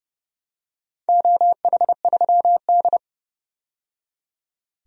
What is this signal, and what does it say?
Morse code, heard as a keyed tone, about 30 wpm, 700 Hz. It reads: OH3D
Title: DXCC-100 Call Signs - 4 Characters - 30wpm